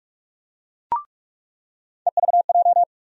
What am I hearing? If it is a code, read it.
Morse code, heard as a keyed tone, about 43 wpm, 700 Hz. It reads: EVJ
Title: USA Call Sign Postfixes - 3 Letters 45wpm